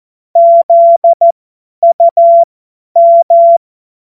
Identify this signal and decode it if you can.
Morse code, heard as a keyed tone, about 14 wpm, spaced as for 10 wpm, 680 Hz. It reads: ZUM
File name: Morsekurs / Witze / Witz 23. Mai 2023 50 F70